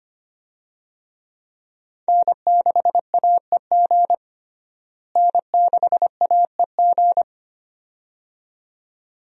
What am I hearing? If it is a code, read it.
Morse code, heard as a keyed tone, about 25 wpm, 700 Hz. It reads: N6AEG N6AEG
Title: USA Call Signs - 5 Characters - Repeated 2x - 25wpm